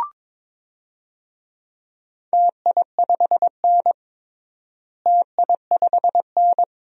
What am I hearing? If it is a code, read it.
Morse code, heard as a keyed tone, about 22 wpm, 700 Hz. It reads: TI5N TI5N
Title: DXCC-100 Call Signs - 4 Characters - Repeated 2x - 22wpm